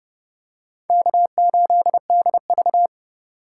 Morse code, heard as a keyed tone, about 27 wpm, 700 Hz. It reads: K8DV